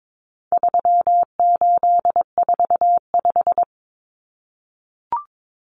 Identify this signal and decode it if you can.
Morse code, heard as a keyed tone, about 22 wpm, 700 Hz. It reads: 3845
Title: Common Cut Numbers - 4-Digit 22wpm